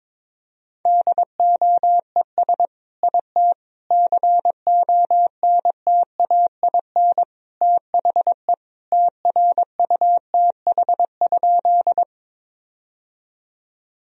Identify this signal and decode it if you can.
Morse code, heard as a keyed tone, about 22 wpm, 700 Hz. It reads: DOES IT CONTAIN THE TRUTH?
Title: Sentences from Top 1000 Words - Review Part 2 - 22wpm